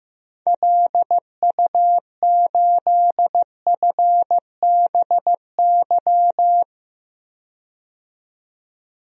Morse code, heard as a keyed tone, about 15 wpm, 690 Hz. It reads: LU8FBY
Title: DXCC-100 Call Signs - 6 Characters - 15wpm